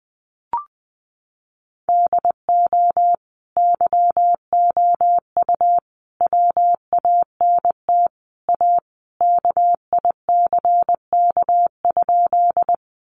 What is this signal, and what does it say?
Morse code, heard as a keyed tone, about 20 wpm, 700 Hz. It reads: DO YOU WANT A KICK?
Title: Sentences from Top 1300 Words 20wpm